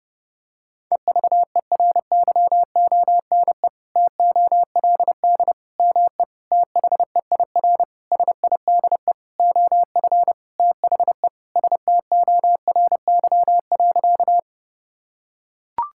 Morse code, heard as a keyed tone, about 30 wpm, 700 Hz. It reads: EVERYONE TOLD ME THEIR SIDE OF THE STORY.
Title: Sentences from Top 1000 Words - Review Part 5 - 30wpm